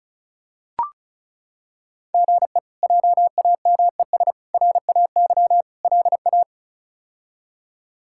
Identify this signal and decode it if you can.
Morse code, heard as a keyed tone, about 36 wpm, 700 Hz. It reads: GE JAMES RAY LA
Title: K1USN SST QSO Practice - 35wpm